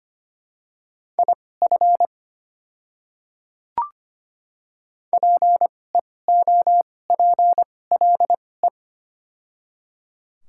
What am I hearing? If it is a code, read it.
Morse code, heard as a keyed tone, about 25 wpm, spaced as for 18 wpm, 700 Hz. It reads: IF PEOPLE
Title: 2x Character Spacing - Top 100 Words - 25wpm